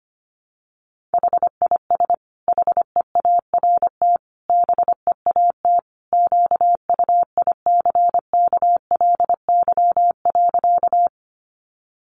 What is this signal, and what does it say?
Morse code, heard as a keyed tone, about 25 wpm, 700 Hz. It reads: HIS HEART BEAT QUICKLY.
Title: Sentences from Top 1300 Words 25wpm